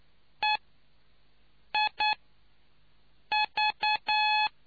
Morse code, heard as a keyed tone, about 9 wpm, spaced as for 5 wpm, 850 Hz. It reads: EIV